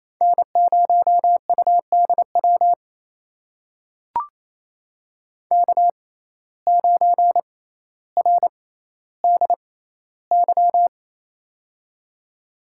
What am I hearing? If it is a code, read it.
Morse code, heard as a keyed tone, about 28 wpm, spaced as for 14 wpm, 700 Hz. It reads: N0UDW K9RDY
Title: USA Call Signs - 5 Characters - 6x Character Spacing - 28wpm